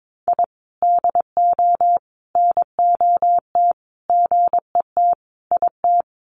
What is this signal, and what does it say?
Morse code, heard as a keyed tone, about 22 wpm, 700 Hz. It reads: I DO NOT GET IT
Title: Sets of 5 Words Encore 22wpm